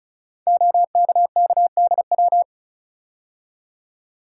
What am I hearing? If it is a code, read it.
Morse code, heard as a keyed tone, about 36 wpm, 700 Hz. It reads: OKKDW
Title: Code Groups - 5 Letter:Numbers - 35wpm